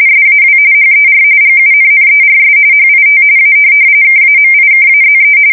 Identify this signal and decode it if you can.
RTTY, about 45.45 baud. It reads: SAW HER GUNNERS HIT A KAMIKAZE
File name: fsk170 / USS-New-Jersey